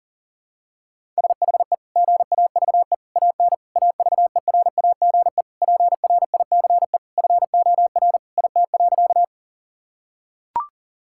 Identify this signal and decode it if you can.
Morse code, heard as a keyed tone, about 40 wpm, 700 Hz. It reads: SHE GAVE AN AVERAGE PRICE FOR IT.
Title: Sentences from Top 1000 Words - Review Part 6 - 40wpm